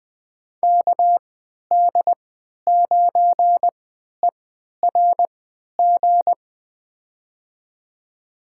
Morse code, heard as a keyed tone, about 20 wpm, spaced as for 11 wpm, 700 Hz. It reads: KD9ERG
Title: USA Call Signs - 6 Characters - 3x Character Spacing - 20wpm